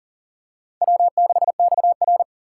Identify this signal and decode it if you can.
Morse code, heard as a keyed tone, about 37 wpm, 700 Hz. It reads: W6XR
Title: K1USN SST QSO Practice - 40wpm